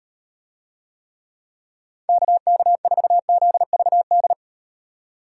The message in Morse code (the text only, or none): KK4ZVD